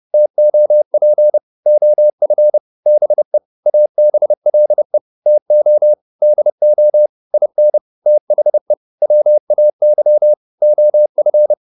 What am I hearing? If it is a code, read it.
Morse code, heard as a keyed tone, about 30 wpm, 600 Hz. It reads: TOP OF BE ABLE TO DO IN THE WAY OF